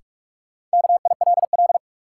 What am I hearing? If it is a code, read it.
Morse code, heard as a keyed tone, about 42 wpm, 700 Hz. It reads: KILL
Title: Sentences from Top 1400 Words - New Word Drill Alt - 45wpm